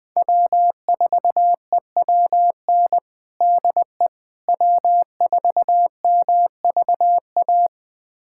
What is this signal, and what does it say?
Morse code, heard as a keyed tone, about 20 wpm, 700 Hz. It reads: W4EWN DE W4MVA
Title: USA Callsign DE Callsign - 5 Characters - 20wpm